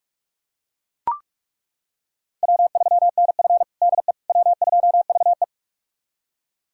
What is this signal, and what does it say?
Morse code, heard as a keyed tone, about 41 wpm, 700 Hz. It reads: W3NF DE W2VE
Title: USA Callsign DE Callsign - 4 Characters - 45wpm